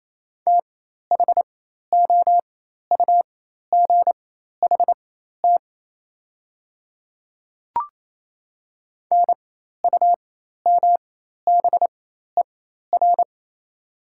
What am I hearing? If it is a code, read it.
Morse code, heard as a keyed tone, about 28 wpm, spaced as for 13 wpm, 690 Hz. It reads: THOUGHT NUMBER
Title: 4x Character Spacing - Top 500 Words - 28wpm